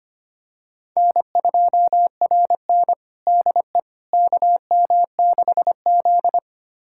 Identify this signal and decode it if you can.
Morse code, heard as a keyed tone, about 25 wpm, 700 Hz. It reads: N2RN DE KM6Z
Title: USA Callsign DE Callsign - 4 Characters - 25wpm